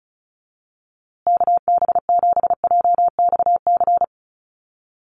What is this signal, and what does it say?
Morse code, heard as a keyed tone, about 34 wpm, 700 Hz. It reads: KB7JXC
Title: USA Call Signs Repeated 3x 35wpm